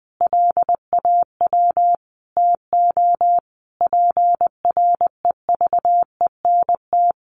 Morse code, heard as a keyed tone, about 20 wpm, 700 Hz. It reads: LAW TO PREVENT
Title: Sentences from Top 1100 Words 20wpm